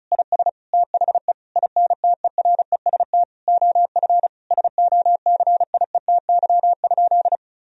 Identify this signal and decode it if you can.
Morse code, heard as a keyed tone, about 36 wpm, 700 Hz. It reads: IS THE INTEREST OF SOCIETY?
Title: Sentences from Top 1000 Words - Review Part 4 - 35wpm